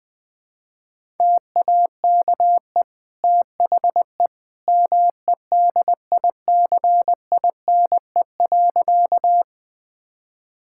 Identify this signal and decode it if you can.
Morse code, heard as a keyed tone, about 20 wpm, 700 Hz. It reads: TAKE THE MEDICINE.
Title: Sentences from Top 1000 Words - Review Part 6 - 20wpm